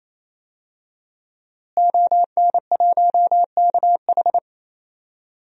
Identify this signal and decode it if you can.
Morse code, heard as a keyed tone, about 28 wpm, 700 Hz. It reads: ON1KH
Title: DXCC-100 Call Signs - 5 Characters - Repeated 3x - 28wpm